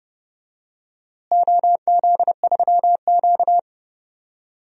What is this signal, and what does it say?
Morse code, heard as a keyed tone, about 29 wpm, 700 Hz. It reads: OZ3Q
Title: DXCC-100 Call Signs - 4 Characters - Repeated 3x - 30wpm